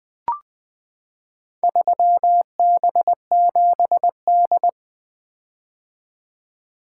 Morse code, heard as a keyed tone, about 20 wpm, 700 Hz. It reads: 3B7D
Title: Four Letter-Number Combinations - 20wpm